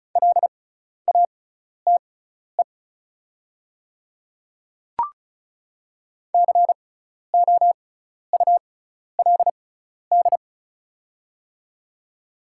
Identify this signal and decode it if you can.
Morse code, heard as a keyed tone, about 36 wpm, spaced as for 12 wpm, 700 Hz. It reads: LATE COULD
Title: 6x Character Spacing - Top 300 Words - 35wpm